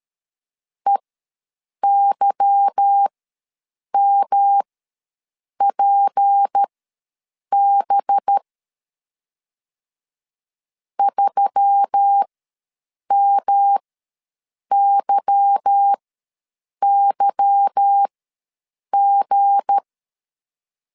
Morse code, heard as a keyed tone, about 13 wpm, spaced as for 7 wpm, 800 Hz. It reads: EYMPB 3MYYG
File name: LEZIONI / 8 YJ45 / 8 Y ws 12wpm 3409